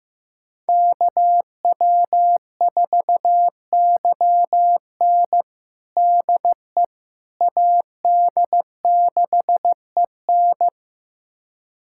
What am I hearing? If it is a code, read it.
Morse code, heard as a keyed tone, about 15 wpm, 700 Hz. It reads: KW4YN DE AD6EN
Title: USA Callsign DE Callsign - 5 Characters - 15wpm